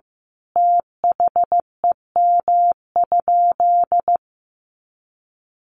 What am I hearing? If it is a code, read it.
Morse code, heard as a keyed tone, about 15 wpm, 700 Hz. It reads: THEM?
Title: Sentences from Top 100 Words 15wpm